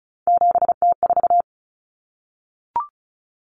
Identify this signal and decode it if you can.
Morse code, heard as a keyed tone, about 34 wpm, 700 Hz. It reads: 7T4
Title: Common Cut Numbers - 3-Digit 35wpm